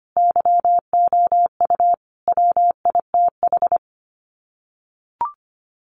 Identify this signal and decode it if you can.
Morse code, heard as a keyed tone, about 25 wpm, 700 Hz. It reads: YOU WITH
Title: Sets of 5 Words Encore2 25wpm